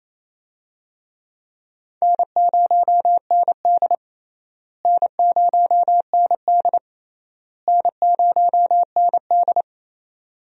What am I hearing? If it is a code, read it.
Morse code, heard as a keyed tone, about 28 wpm, 700 Hz. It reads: N0ND N0ND N0ND
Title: USA Call Signs - 4 Characters - Repeated 3x - 28wpm